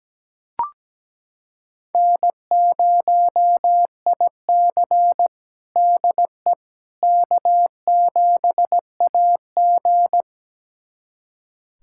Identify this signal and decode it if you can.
Morse code, heard as a keyed tone, about 17 wpm, 700 Hz. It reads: N0IC DE K7AG
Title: USA Callsign DE Callsign - 4 Characters - 17wpm